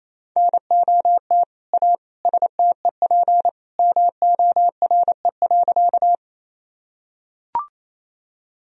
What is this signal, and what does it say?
Morse code, heard as a keyed tone, about 28 wpm, 700 Hz. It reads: NOT A STEP MORE.
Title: Sentences from Top 1000 Words - Review Part 2 - 28wpm